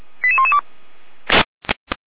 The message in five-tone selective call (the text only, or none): 88112